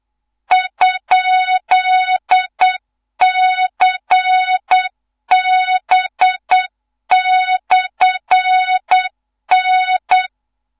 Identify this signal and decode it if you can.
Morse code, heard as a keyed tone, about 8 wpm, 3700 Hz, with noise. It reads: ?CB/N
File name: Morze-HA2MN / hang / 15 GYAKORLAT